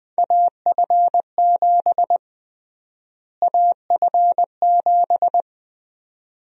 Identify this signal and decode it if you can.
Morse code, heard as a keyed tone, about 21 wpm, 700 Hz. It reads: AF7 AF7
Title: USA Call Sign Prefixes - Repeated 2x - 20wpm